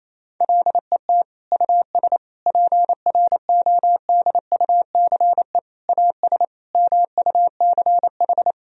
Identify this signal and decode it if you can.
Morse code, heard as a keyed tone, about 28 wpm, 700 Hz. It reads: LET US PRODUCE AS MUCH
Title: Sentences from Top 1000 Words - Review Part 3 - 28wpm